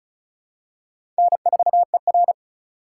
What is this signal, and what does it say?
Morse code, heard as a keyed tone, about 34 wpm, 700 Hz. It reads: N4ER
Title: USA Call Signs - 4 Characters - 4x Character Spacing - 35wpm